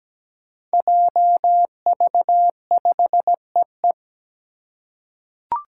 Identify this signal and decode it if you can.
Morse code, heard as a keyed tone, about 17 wpm, 700 Hz. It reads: JV5EE
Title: Five Challenging Letter-Number Combinations 17wpm